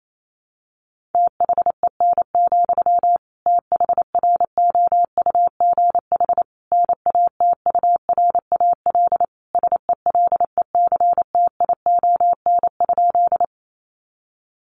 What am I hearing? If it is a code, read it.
Morse code, heard as a keyed tone, about 28 wpm, 700 Hz. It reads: THEN, THROUGH NATURAL SELECTION?
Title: Sentences from Top 1400 Words 28wpm